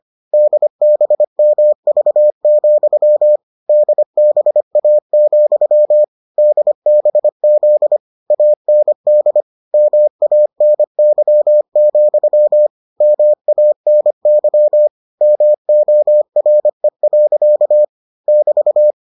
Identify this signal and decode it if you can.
Morse code, heard as a keyed tone, about 25 wpm, 600 Hz. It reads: DBMV, DBA, DBZ AND MANY, MANY MORE. <BT>